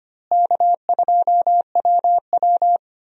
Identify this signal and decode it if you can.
Morse code, heard as a keyed tone, about 25 wpm, 700 Hz. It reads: K2WW
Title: USA Call Signs - 4 Characters - Repeated 2x - 25wpm